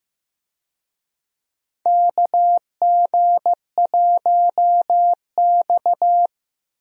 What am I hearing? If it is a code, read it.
Morse code, heard as a keyed tone, about 16 wpm, 700 Hz. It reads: KG1X
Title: USA Call Signs - 4 Characters - Repeated 2x - 15wpm